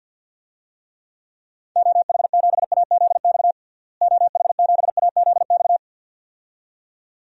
Morse code, heard as a keyed tone, about 52 wpm, 700 Hz. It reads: OH7AZX OH7AZX
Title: DXCC-100 Call Signs - 6 Characters - Repeated 2x - 50wpm